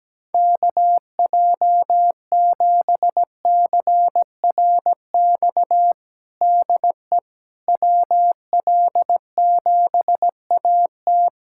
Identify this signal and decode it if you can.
Morse code, heard as a keyed tone, about 17 wpm, 700 Hz. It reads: KJ7CRX DE WL7AT
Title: USA Callsign DE Callsign 17wpm